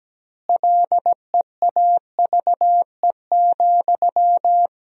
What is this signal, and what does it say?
Morse code, heard as a keyed tone, about 17 wpm, 700 Hz. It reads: LEAVE,
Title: 2x Word-Spacing - Sentences from Top 400 Words - 17wpm